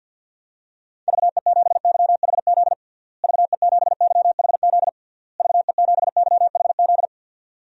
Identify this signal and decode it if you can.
Morse code, heard as a keyed tone, about 53 wpm, 700 Hz. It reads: VE7YHZ VE7YHZ VE7YHZ
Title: DXCC-100 Call Signs - 6 Characters - Repeated 3x - 50wpm